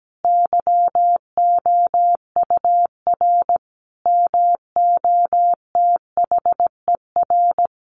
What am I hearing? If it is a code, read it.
Morse code, heard as a keyed tone, about 17 wpm, 700 Hz. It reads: YOUR MOTHER
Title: Sentences from Top 1100 Words 17wpm